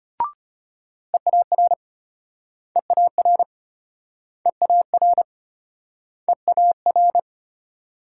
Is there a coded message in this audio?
Yes: Morse code from 1.1 to 7.2 s, heard as a keyed tone, about 34 wpm, 700 Hz. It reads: EAR EAR EAR EAR